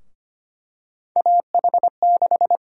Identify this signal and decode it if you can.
Morse code, heard as a keyed tone, about 25 wpm, 700 Hz. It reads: AH6
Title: USA Call Sign Prefixes - Repeated 2x - 25wpm